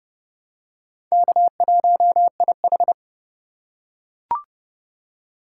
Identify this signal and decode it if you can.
Morse code, heard as a keyed tone, about 26 wpm, 700 Hz. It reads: K1IH